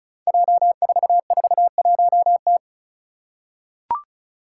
Morse code, heard as a keyed tone, about 36 wpm, 700 Hz. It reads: J441T
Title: Five Challenging Letter-Number Combinations 35wpm